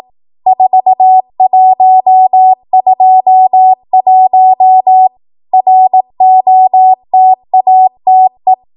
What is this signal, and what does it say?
Morse code, heard as a keyed tone, about 18 wpm, 750 Hz. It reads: 4121 ROTATE
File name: Morse / 18WPM